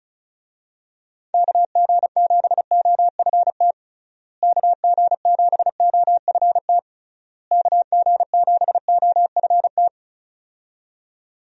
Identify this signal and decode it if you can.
Morse code, heard as a keyed tone, about 34 wpm, 700 Hz. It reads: KG7OFT KG7OFT KG7OFT
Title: USA Call Signs - 6 Characters - Repeated 3x - 35wpm